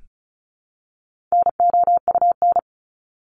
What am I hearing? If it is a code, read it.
Morse code, heard as a keyed tone, about 34 wpm, 700 Hz. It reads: NOUN
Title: Top 500 Words 35wpm